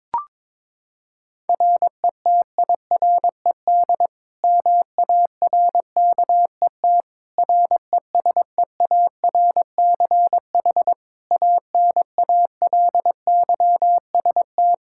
Morse code, heard as a keyed tone, about 22 wpm, 700 Hz. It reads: RETIRED MARKET RESEARCH ANALYST
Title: Practice QSO - Exchange 3 - More Info - 22wpm